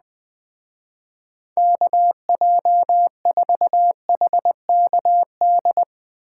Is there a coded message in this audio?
Yes: Morse code from 1.6 to 5.8 s, heard as a keyed tone, about 20 wpm, 700 Hz. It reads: KJ4HKD